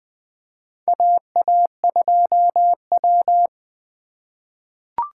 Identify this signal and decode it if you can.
Morse code, heard as a keyed tone, about 21 wpm, 700 Hz. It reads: AA2W
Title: USA Call Signs - 2x Character Spacing - 20wpm